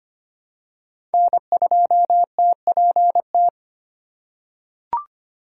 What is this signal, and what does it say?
Morse code, heard as a keyed tone, about 25 wpm, 700 Hz. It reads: N2TPT
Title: USA Call Signs - 4x Character Spacing - 25wpm